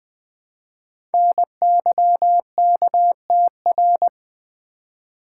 Morse code, heard as a keyed tone, about 20 wpm, 700 Hz. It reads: NYKTR
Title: Code Groups - 5 Letters - 20wpm